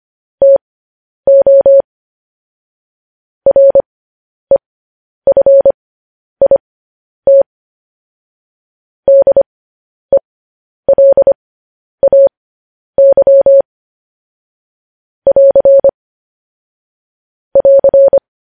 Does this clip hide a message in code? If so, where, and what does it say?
Yes: Morse code from 0.4 to 18.2 s, heard as a keyed tone, about 25 wpm, spaced as for 10 wpm, 550 Hz. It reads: TO REFIT DELAY <AR> <AR>